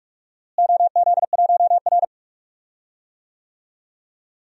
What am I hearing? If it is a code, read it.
Morse code, heard as a keyed tone, about 45 wpm, 700 Hz. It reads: OZ1R
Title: DXCC-100 Call Signs - 4 Characters - Repeated 3x - 45wpm